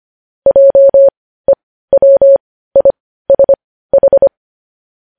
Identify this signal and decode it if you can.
Morse code, heard as a keyed tone, about 25 wpm, spaced as for 15 wpm, 550 Hz. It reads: JEWISH
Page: News Headlines in Morse Code at 15 WPM